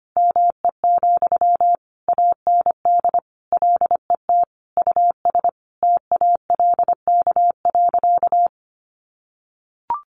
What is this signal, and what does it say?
Morse code, heard as a keyed tone, about 25 wpm, 700 Hz. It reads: ME, AND LET US TALK.
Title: Sentences from Top 1300 Words 25wpm